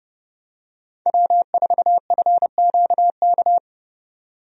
Morse code, heard as a keyed tone, about 30 wpm, 700 Hz. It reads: W4FQK
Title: USA Call Signs - 5 Characters - 30wpm